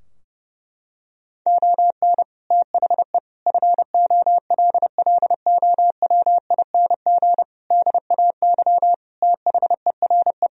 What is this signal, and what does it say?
Morse code, heard as a keyed tone, about 30 wpm, 700 Hz. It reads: ON THE FOLLOWING DAY THERE
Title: Sentences from Top 1000 Words - Review Part 4 - 30wpm